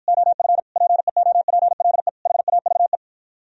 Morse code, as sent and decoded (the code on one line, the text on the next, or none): --- ..-. / .--. --- .--. .-.. / .... .- ...- .
OF PEOPLE HAVE